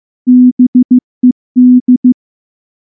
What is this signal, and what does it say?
Morse code, heard as a keyed tone, about 14 wpm, 250 Hz. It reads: BED
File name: morse / morse-training-15WPM